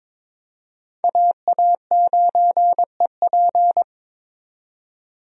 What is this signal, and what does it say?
Morse code, heard as a keyed tone, about 22 wpm, 700 Hz. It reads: AA9EP
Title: USA Call Signs - 5 Characters - 22wpm